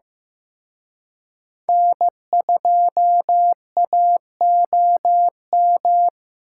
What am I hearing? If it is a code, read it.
Morse code, heard as a keyed tone, about 18 wpm, 700 Hz. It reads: N2AOM
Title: Speed Racing - USA Call Signs 15wpm